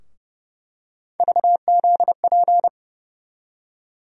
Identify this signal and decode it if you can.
Morse code, heard as a keyed tone, about 30 wpm, 700 Hz. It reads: VZP
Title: USA Call Sign Postfixes - 3 Letters 30wpm